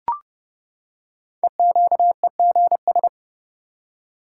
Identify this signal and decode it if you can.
Morse code, heard as a keyed tone, about 30 wpm, 690 Hz. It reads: EQEGS
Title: Code Groups - 5 Letters - 30wpm